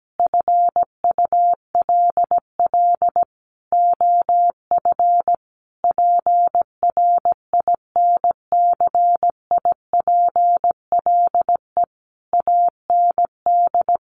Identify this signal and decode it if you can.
Morse code, heard as a keyed tone, about 17 wpm, 700 Hz. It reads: FULL OF PRINCIPLE AND
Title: Sentences from Top 1400 Words 17wpm